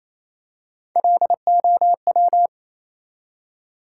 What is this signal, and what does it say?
Morse code, heard as a keyed tone, about 28 wpm, 700 Hz. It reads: LOW
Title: Three Letter Words - 28wpm